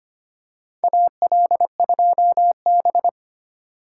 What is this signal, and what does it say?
Morse code, heard as a keyed tone, about 25 wpm, 700 Hz. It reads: AL2B
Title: USA Call Signs - 4 Characters - 3x Character Spacing - 25wpm